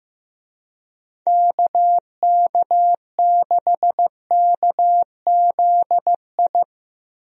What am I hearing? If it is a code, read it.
Morse code, heard as a keyed tone, about 15 wpm, 700 Hz. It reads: KK6KZI
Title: USA Call Signs - 6 Characters - 15wpm